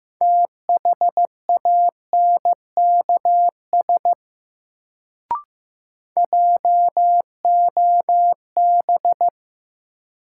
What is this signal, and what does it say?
Morse code, heard as a keyed tone, about 15 wpm, 700 Hz. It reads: THANKS JOB
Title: Sentences from Top 600 Words - New Word Drill - 15wpm